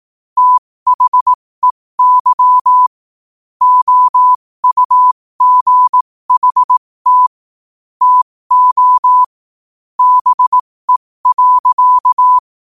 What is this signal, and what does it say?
Morse code, heard as a keyed tone, about 18 wpm, spaced as for 15 wpm, 1000 Hz. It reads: THEY OUGHT TO BE.
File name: 15wpm